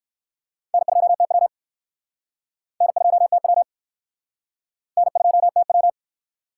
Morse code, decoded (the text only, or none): N2TW N2TW N2TW